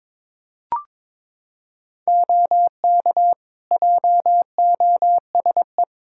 Morse code, heard as a keyed tone, about 22 wpm, 700 Hz. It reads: OK JOSE
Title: K1USN SST QSO Practice 22wpm